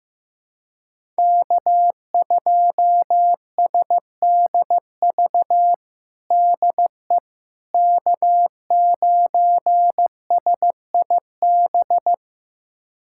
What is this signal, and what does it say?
Morse code, heard as a keyed tone, about 15 wpm, 700 Hz. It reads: K2SDV DE K9SIB
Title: USA Callsign DE Callsign - 5 Characters - 15wpm